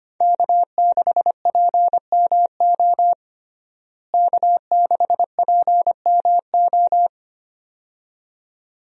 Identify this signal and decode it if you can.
Morse code, heard as a keyed tone, about 25 wpm, 700 Hz. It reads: K6PMO K6PMO
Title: USA Call Signs - 5 Characters - Repeated 2x - 25wpm